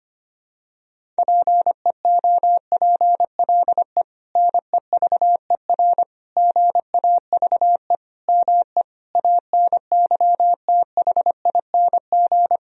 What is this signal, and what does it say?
Morse code, heard as a keyed tone, about 25 wpm, 700 Hz. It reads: PEOPLE NEVER GAVE ME ANYTHING
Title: Sentences from Top 1000 Words - Review Part 1 - 25wpm